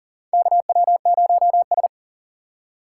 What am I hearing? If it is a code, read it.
Morse code, heard as a keyed tone, about 38 wpm, 700 Hz. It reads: KW0S
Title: USA Call Signs - 4 Characters - 6x Character Spacing - 40wpm